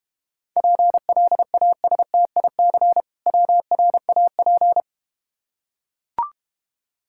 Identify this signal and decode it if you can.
Morse code, heard as a keyed tone, about 31 wpm, 700 Hz. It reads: PLASTIC WRAP
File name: Easy Words - Speed-Racing with Hints - 25wpm